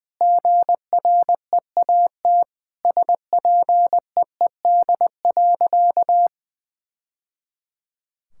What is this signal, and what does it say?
Morse code, heard as a keyed tone, about 20 wpm, 700 Hz. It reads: GREAT SPEED.
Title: Sentences from Top 1000 Words - Review Part 5 - 20wpm